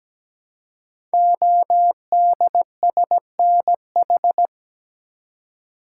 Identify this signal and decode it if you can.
Morse code, heard as a keyed tone, about 17 wpm, 700 Hz. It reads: ODSNH
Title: Code Groups - 5 Letter:Numbers - 17wpm